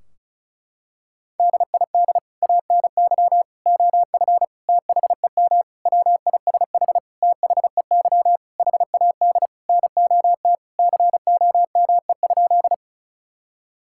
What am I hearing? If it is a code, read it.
Morse code, heard as a keyed tone, about 36 wpm, 700 Hz. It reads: DID ANY OF THEM WISH THEY HAD NOT COME?
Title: Sentences from Top 1600 Words 35wpm